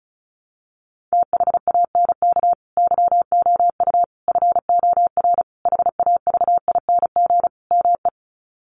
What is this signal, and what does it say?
Morse code, heard as a keyed tone, about 34 wpm, 700 Hz. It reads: THANK YOU FOR HAVING ME
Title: Sets of 5 Words Encore 35wpm